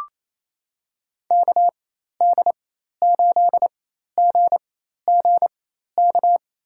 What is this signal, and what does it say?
Morse code, heard as a keyed tone, about 28 wpm, spaced as for 13 wpm, 700 Hz. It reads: KD8GGK
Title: USA Call Signs - 6 Characters - 4x Character Spacing - 28wpm